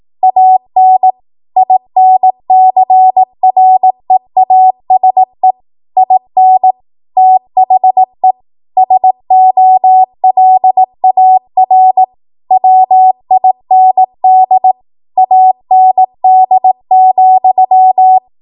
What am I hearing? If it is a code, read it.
Morse code, heard as a keyed tone, about 18 wpm, 750 Hz. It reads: AN INCREASE IN THE SOLAR WIND AND,